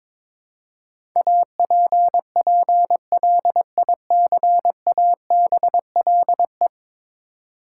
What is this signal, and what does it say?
Morse code, heard as a keyed tone, about 22 wpm, 700 Hz. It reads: APPLICABLE
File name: practice / 22wpm / Ten Letter Words - 22wpm